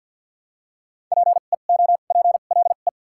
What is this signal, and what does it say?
Morse code, heard as a keyed tone, about 54 wpm, 700 Hz. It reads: P E O P L E
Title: 2x Character Spacing - Top 100 Words - 50wpm